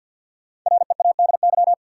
Morse code, heard as a keyed tone, about 53 wpm, 700 Hz. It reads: READY
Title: Sentences from Top 400 Words - New Word Drill - 50wpm